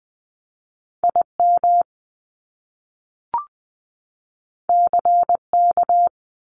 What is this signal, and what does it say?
Morse code, heard as a keyed tone, about 20 wpm, 700 Hz. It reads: IM CK
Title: Top Two Letter Combinations in Words 20wpm